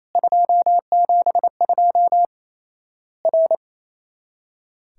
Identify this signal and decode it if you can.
Morse code, heard as a keyed tone, about 28 wpm, 700 Hz. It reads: 272 R
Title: Worked All Europe DX Contest - QTC Exchange - 28wpm